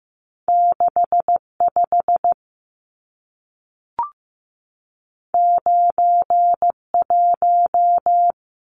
Morse code, heard as a keyed tone, about 15 wpm, 700 Hz. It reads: 6 5 9 1
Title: Numbers - 2-Digit 15wpm